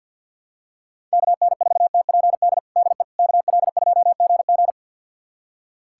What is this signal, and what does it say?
Morse code, heard as a keyed tone, about 51 wpm, 700 Hz. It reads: KN4TPD DE KL2GG
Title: USA Callsign DE Callsign 50wpm